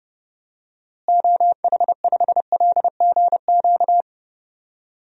Morse code, heard as a keyed tone, about 30 wpm, 700 Hz. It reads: OH5LGQ
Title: DXCC-100 Call Signs - 6 Characters - Repeated 2x - 30wpm